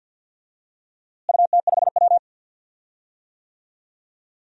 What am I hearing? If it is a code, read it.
Morse code, heard as a keyed tone, about 51 wpm, 700 Hz. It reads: UT5W
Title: DXCC-100 Call Signs - 4 Characters - Repeated 2x - 50wpm